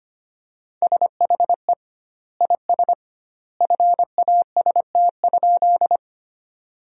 Morse code, heard as a keyed tone, about 25 wpm, 700 Hz. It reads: SHE IS FAST?
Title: 2x Word-Spacing - Sentences from Top 400 Words - 25wpm